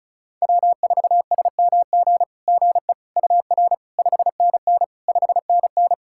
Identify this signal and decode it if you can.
Morse code, heard as a keyed tone, about 34 wpm, 690 Hz. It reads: W4SMG GE UR 5NN 5NN
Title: SOTA QSO Practice 35wpm